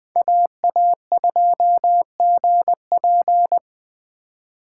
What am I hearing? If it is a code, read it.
Morse code, heard as a keyed tone, about 20 wpm, 700 Hz. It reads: AA2GP
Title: USA Call Signs - 5 Characters - 4x Character Spacing - 20wpm